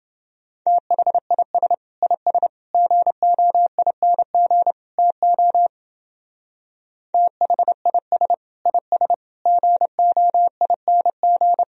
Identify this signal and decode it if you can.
Morse code, heard as a keyed tone, about 30 wpm, 700 Hz. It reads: THIS IS GOING TO THIS IS GOING